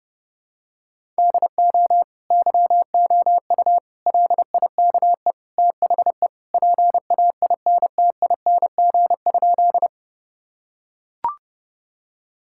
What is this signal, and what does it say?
Morse code, heard as a keyed tone, about 30 wpm, 700 Hz. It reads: DO YOU LIKE THE PAINTING?
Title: Sentences from Top 1000 Words - Review Part 6 - 30wpm